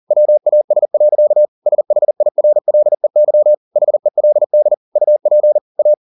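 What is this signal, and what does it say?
Morse code, heard as a keyed tone, about 40 wpm, 600 Hz. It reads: WAS. SHIRLEY HELD UP A
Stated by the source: Morse code at 40 wpm, practice text, CW med Gnister.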